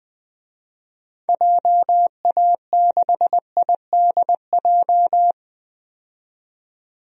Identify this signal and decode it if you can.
Morse code, heard as a keyed tone, about 20 wpm, 700 Hz. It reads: JA6IDJ
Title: DXCC-100 Call Signs - 6 Characters - 20wpm